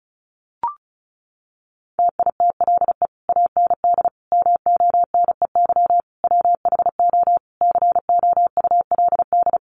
Morse code, heard as a keyed tone, about 34 wpm, 700 Hz. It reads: TITLE AND MONEY WHO COULD
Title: Sentences from Top 1200 Words 35wpm